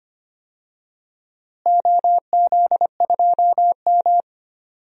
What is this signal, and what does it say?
Morse code, heard as a keyed tone, about 25 wpm, 700 Hz. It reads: OZ2M
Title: DXCC-100 Call Signs - 4 Characters - Repeated 2x - 25wpm